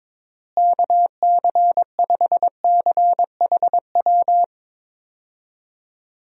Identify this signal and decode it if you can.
Morse code, heard as a keyed tone, about 22 wpm, 700 Hz. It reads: KC5CHW
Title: USA Call Signs - 6 Characters - Repeated 3x - 22wpm